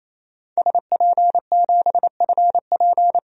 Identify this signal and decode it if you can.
Morse code, heard as a keyed tone, about 28 wpm, 700 Hz. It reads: SP7FP
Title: DXCC-100 Call Signs - 5 Characters - 28wpm